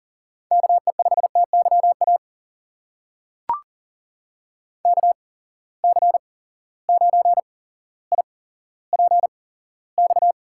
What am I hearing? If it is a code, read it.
Morse code, heard as a keyed tone, about 40 wpm, spaced as for 17 wpm, 700 Hz. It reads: KE5TYA KC9IPX
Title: USA Call Signs - 6 Characters - 8x Character Spacing - 40wpm